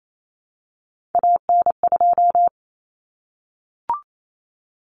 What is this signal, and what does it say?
Morse code, heard as a keyed tone, about 28 wpm, 700 Hz. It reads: AN2
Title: Common Cut Numbers - 3-Digit 28wpm